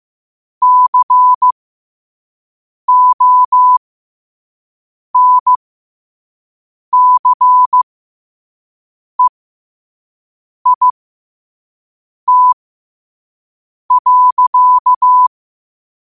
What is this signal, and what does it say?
Morse code, heard as a keyed tone, about 15 wpm, spaced as for 5 wpm, 1000 Hz. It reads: CONCEIT.
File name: qotdfiles / 5wpm